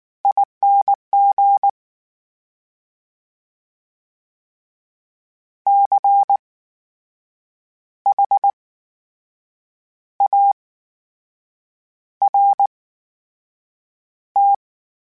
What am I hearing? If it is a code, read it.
Morse code, heard as a keyed tone, about 19 wpm, spaced as for 6 wpm, 800 Hz. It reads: ING CHART
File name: Special INGs / Special INGs-mp3s / VerySlowToFaster Ing @18